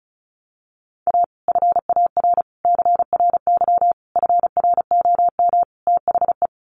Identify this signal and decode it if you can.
Morse code, heard as a keyed tone, about 34 wpm, 700 Hz. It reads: A FAR CRY FROM THE
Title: Sets of 5 Words Encore 35wpm